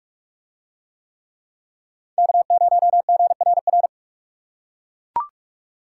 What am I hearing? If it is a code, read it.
Morse code, heard as a keyed tone, about 40 wpm, 700 Hz. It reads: K0GRR